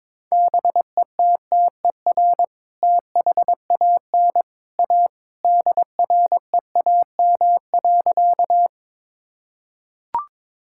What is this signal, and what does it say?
Morse code, heard as a keyed tone, about 23 wpm, 700 Hz. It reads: BETTER THAN A DREAM.
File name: practice / 22wpm / Sentences with 2-Word Phrases - 22wpm